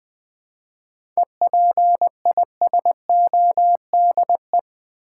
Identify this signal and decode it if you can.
Morse code, heard as a keyed tone, about 20 wpm, 700 Hz. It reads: EPISODE
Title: 7 Letter Words - First 3 Letters Given - 20wpm